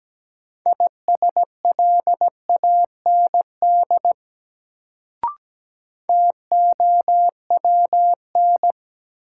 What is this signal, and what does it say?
Morse code, heard as a keyed tone, about 17 wpm, 700 Hz. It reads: ISLAND TOWN
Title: Sentences from Top 500 Words - New Word Drill - 17wpm